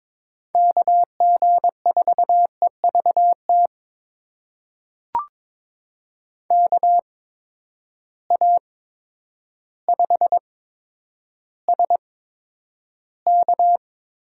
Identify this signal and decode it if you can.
Morse code, heard as a keyed tone, about 22 wpm, spaced as for 10 wpm, 700 Hz. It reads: KG4EVT KA5SK
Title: USA Call Signs - 6 Characters - 8x Character Spacing - 22wpm